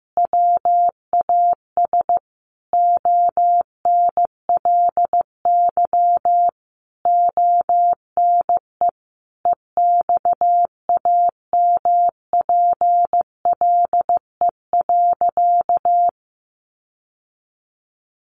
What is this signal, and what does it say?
Morse code, heard as a keyed tone, about 15 wpm, 700 Hz. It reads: WAS ONLY ONE EXAMPLE.
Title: Sentences from Top 1500 Words 15wpm